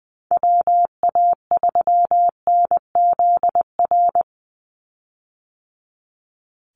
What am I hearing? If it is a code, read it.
Morse code, heard as a keyed tone, about 20 wpm, 700 Hz. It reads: WA3NZR
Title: Call Signs - 20wpm